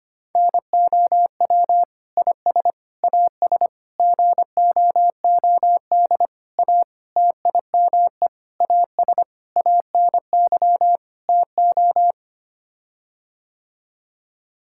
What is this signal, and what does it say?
Morse code, heard as a keyed tone, about 25 wpm, 700 Hz. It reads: NOW IS AS GOOD A TIME AS ANY TO
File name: Sets of 9 Words - 25wpm